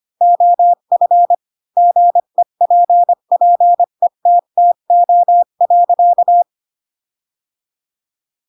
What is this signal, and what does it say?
Morse code, heard as a keyed tone, about 25 wpm, 700 Hz. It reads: OF GEPPETTO.